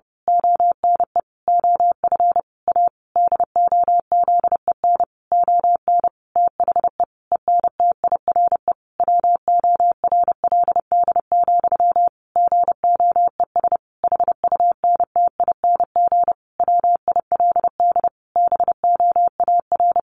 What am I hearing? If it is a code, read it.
Morse code, heard as a keyed tone, about 30 wpm, 700 Hz. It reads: ONE OF A DOZEN ON THE ENTIRE WORLD, GOES HUNTING WILD BOAR
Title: A Sound Of Thunder 30wpm